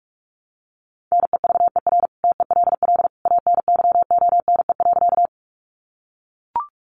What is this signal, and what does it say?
Morse code, heard as a keyed tone, about 44 wpm, 700 Hz. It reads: NEVER TELL ANYONE.
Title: Sentences from Top 900 Words 45wpm